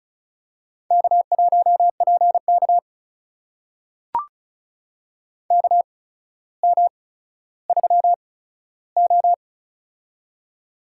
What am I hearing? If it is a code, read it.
Morse code, heard as a keyed tone, about 34 wpm, spaced as for 15 wpm, 700 Hz. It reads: K1PK KM3O
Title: USA Call Signs - 4 Characters - 8x Character Spacing - 35wpm